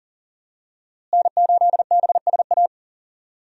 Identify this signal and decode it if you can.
Morse code, heard as a keyed tone, about 40 wpm, 700 Hz. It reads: N8BSA
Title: USA Call Signs - 5 Characters - Repeated 3x - 40wpm